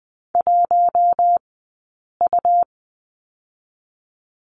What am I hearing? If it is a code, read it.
Morse code, heard as a keyed tone, about 20 wpm, 700 Hz. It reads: 1 U
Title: 099 Lesson 17d - Letter u - Two Characters 20wpm